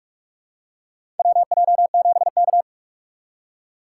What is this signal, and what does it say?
Morse code, heard as a keyed tone, about 44 wpm, 700 Hz. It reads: WJ7K
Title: USA Call Signs - 4 Characters - 4x Character Spacing - 45wpm